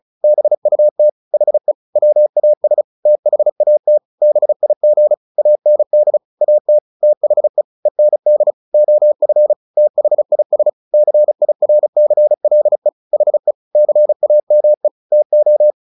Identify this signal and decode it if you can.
Morse code, heard as a keyed tone, about 34 wpm, 600 Hz. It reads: BUT HE WAS THAT BIG AND AT THE END OF THIS CIRCLE HE CAME TO
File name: the old man and the sea 2 35wpm